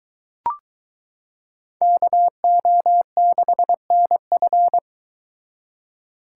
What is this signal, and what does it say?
Morse code, heard as a keyed tone, about 22 wpm, 700 Hz. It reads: KO6NF